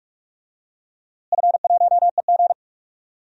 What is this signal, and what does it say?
Morse code, heard as a keyed tone, about 41 wpm, 700 Hz. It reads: F1EG
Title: DXCC-100 Call Signs - 4 Characters - Repeated 2x - 45wpm